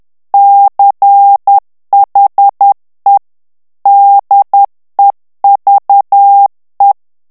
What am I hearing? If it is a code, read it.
Morse code, heard as a keyed tone, about 10 wpm, 800 Hz. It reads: CHE DEVE
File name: morse mp3 / ITALIEN texte clair 1 vitesse 10 mots minute